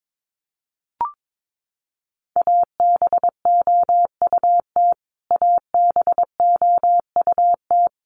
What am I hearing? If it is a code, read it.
Morse code, heard as a keyed tone, about 22 wpm, 700 Hz. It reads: ABOUT ABOUT
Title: Top 300 Words Repeated 22wpm